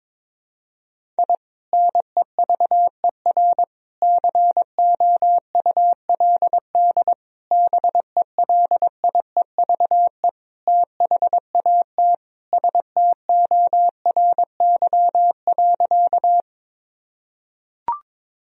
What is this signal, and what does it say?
Morse code, heard as a keyed tone, about 22 wpm, 700 Hz. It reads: I NEVER COULD BELIEVE THAT STORY.
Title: Sentences from Top 1000 Words - Review Part 1 - 22wpm